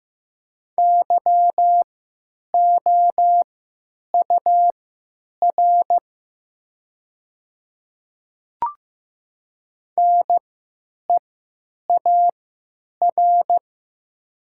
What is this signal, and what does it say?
Morse code, heard as a keyed tone, about 15 wpm, spaced as for 9 wpm, 700 Hz. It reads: YOUR NEAR
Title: 3x Character Spacing - Top 200 Words - 15wpm